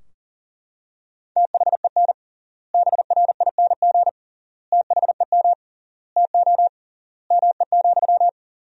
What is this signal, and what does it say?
Morse code, heard as a keyed tone, about 40 wpm, 700 Hz. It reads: THEN BRING THEM TO ME,
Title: 3x Word-Spacing - Sentences from Top 500 Words - 40wpm